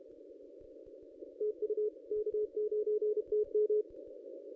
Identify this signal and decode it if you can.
Morse code, heard as a keyed tone, about 32 wpm, 440 Hz. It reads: TU K9TM
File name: contests / ss-cw-k5tr-op-n5rz -1100